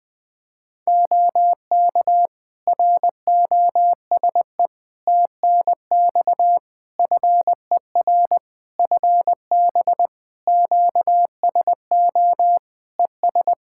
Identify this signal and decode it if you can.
Morse code, heard as a keyed tone, about 20 wpm, 700 Hz. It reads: OK ROSE TNX FER FB QSO ES
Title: Practice QSO - Exchange 4 - Ending - 20wpm